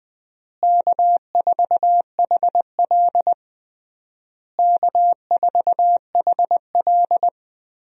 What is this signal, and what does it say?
Morse code, heard as a keyed tone, about 20 wpm, 700 Hz. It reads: K4HL K4HL
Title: USA Call Signs - 4 Characters - Repeated 3x - 20wpm